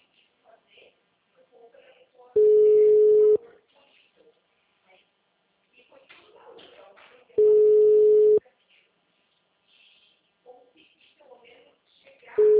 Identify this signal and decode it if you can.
Ringback tone (Europe, 425 Hz), with speech.
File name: out- -2041- - - .57183